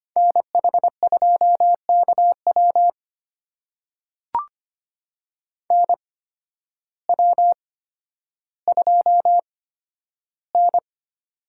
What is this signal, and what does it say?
Morse code, heard as a keyed tone, about 25 wpm, spaced as for 12 wpm, 700 Hz. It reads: NH2KW NW2N
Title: USA Call Signs - 8x Character Spacing - 25wpm